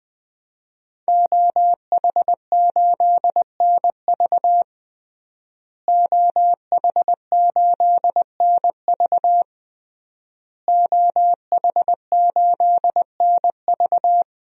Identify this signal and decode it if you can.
Morse code, heard as a keyed tone, about 20 wpm, 700 Hz. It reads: OH8NV OH8NV OH8NV
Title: DXCC-100 Call Signs - 5 Characters - Repeated 3x - 20wpm